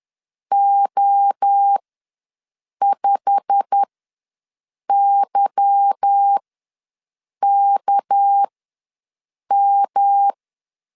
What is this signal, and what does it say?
Morse code, heard as a keyed tone, about 10 wpm, spaced as for 6 wpm, 800 Hz. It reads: O5YKM
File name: LEZIONI / 8 YJ45 / 8 YJ45 ws 10wpm 3843